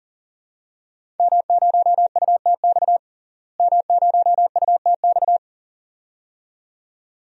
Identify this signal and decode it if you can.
Morse code, heard as a keyed tone, about 40 wpm, 700 Hz. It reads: M0UTX M0UTX
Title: DXCC-100 Call Signs - 5 Characters - Repeated 2x - 40wpm